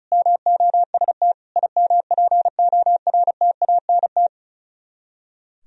Morse code, heard as a keyed tone, about 42 wpm, 700 Hz. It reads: MOST IMPORTANT
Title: Speed-Racing - Sets of 3 Words 35wpm